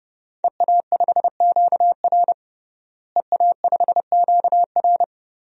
Morse code, heard as a keyed tone, about 30 wpm, 700 Hz. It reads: EA5QR EA5QR
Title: DXCC-100 Call Signs - 5 Characters - Repeated 2x - 30wpm